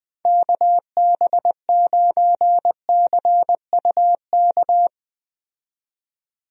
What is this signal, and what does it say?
Morse code, heard as a keyed tone, about 20 wpm, 700 Hz. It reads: KB9CUK
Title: USA Call Signs - 6 Characters - 20wpm